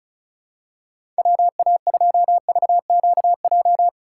Morse code, heard as a keyed tone, about 36 wpm, 700 Hz. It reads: WA2VQJ
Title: USA Call Signs - 6 Characters - 3x Character Spacing - 35wpm